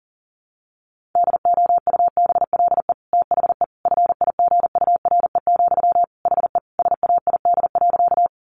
Morse code, heard as a keyed tone, about 40 wpm, 700 Hz. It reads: DOUBLE THE FIGURE, HE SAID.
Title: Sentences from Top 1400 Words 40wpm